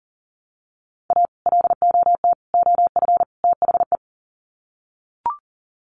Morse code, heard as a keyed tone, about 40 wpm, 700 Hz. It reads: A LOT OF THE
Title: Sets of 4 Words 40wpm